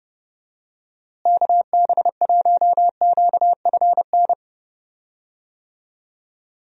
Morse code, heard as a keyed tone, about 30 wpm, 700 Hz. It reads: KB1QFN
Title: USA Call Signs - 6 Characters - Repeated 2x - 30wpm